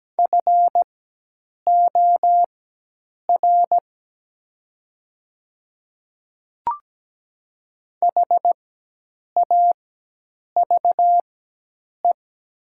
Morse code, heard as a keyed tone, about 17 wpm, spaced as for 8 wpm, 700 Hz. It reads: FOR HAVE
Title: 4x Character Spacing - Top 300 Words - 17wpm